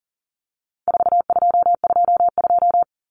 Morse code, heard as a keyed tone, about 40 wpm, 700 Hz. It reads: 4222
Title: Common Cut Numbers - 4-Digit 40wpm